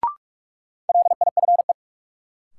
Morse code, heard as a keyed tone, about 46 wpm, 700 Hz. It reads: LIFE
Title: Sentences from Top 300 Words - New Word Drill - 45wpm